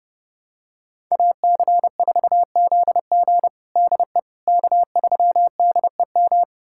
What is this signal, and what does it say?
Morse code, heard as a keyed tone, about 30 wpm, 700 Hz. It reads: AC4ZG DE K3DEM
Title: USA Callsign DE Callsign - 5 Characters - 30wpm